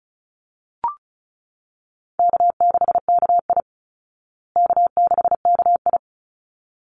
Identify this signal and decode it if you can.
Morse code, heard as a keyed tone, about 34 wpm, 700 Hz. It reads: K6KI K6KI
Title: Easy Call Signs Repeated 2x - 35wpm